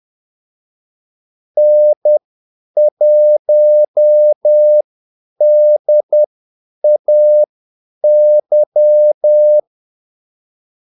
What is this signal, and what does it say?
Morse code, heard as a keyed tone, about 10 wpm, spaced as for 8 wpm, 600 Hz. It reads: N1DAY
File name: Callsigns-10-8wpm